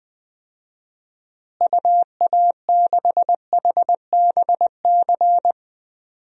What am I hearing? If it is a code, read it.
Morse code, heard as a keyed tone, about 20 wpm, 700 Hz. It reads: UA6HBC
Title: DXCC-100 Call Signs - 6 Characters - Repeated 2x - 20wpm